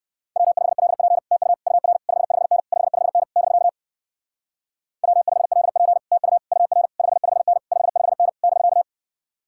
Morse code, heard as a keyed tone, about 56 wpm, 700 Hz. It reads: W5LP TU UR 55N 55N <BK> W5LP TU UR 55N 55N <BK>
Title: Speed-Racing Common QSO Phrases - 45wpm